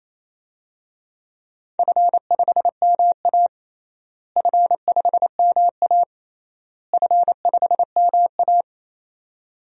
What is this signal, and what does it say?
Morse code, heard as a keyed tone, about 28 wpm, 700 Hz. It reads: F5MA F5MA F5MA
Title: DXCC-100 Call Signs - 4 Characters - Repeated 3x - 28wpm